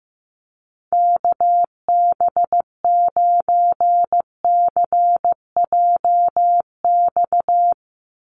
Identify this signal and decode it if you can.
Morse code, heard as a keyed tone, about 15 wpm, 700 Hz. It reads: KB9CJX
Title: USA Call Signs 15wpm